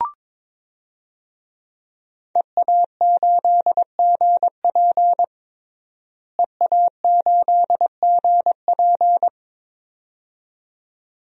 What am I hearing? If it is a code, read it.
Morse code, heard as a keyed tone, about 22 wpm, 700 Hz. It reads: EA8GP EA8GP
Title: DXCC-100 Call Signs - 5 Characters - Repeated 2x - 22wpm